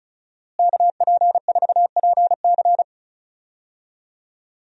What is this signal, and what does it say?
Morse code, heard as a keyed tone, about 34 wpm, 690 Hz. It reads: KP4PC
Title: DXCC-100 Call Signs - 5 Characters - Repeated 3x - 35wpm